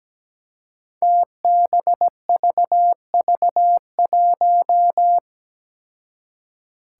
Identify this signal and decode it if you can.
Morse code, heard as a keyed tone, about 17 wpm, 700 Hz. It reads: TBVV1
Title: Five Challenging Letter-Number Combinations 17wpm